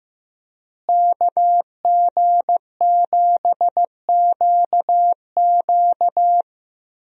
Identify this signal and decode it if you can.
Morse code, heard as a keyed tone, about 15 wpm, 700 Hz. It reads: KG7QQ
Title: USA Call Signs - 5 Characters - Repeated 2x - 15wpm